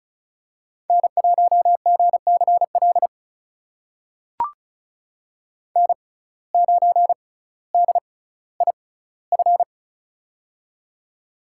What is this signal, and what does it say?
Morse code, heard as a keyed tone, about 34 wpm, spaced as for 18 wpm, 700 Hz. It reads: N1GCL N9DIF
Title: USA Call Signs - 5 Characters - 6x Character Spacing - 35wpm